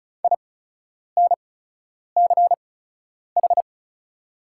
Morse code, heard as a keyed tone, about 36 wpm, spaced as for 10 wpm, 700 Hz. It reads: INCH
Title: 8x Character Spacing - Top 500 Words - 35wpm